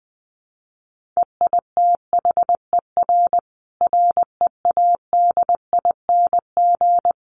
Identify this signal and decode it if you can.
Morse code, heard as a keyed tone, about 20 wpm, 700 Hz. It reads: EITHER READING
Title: Sentences from Top 1400 Words 20wpm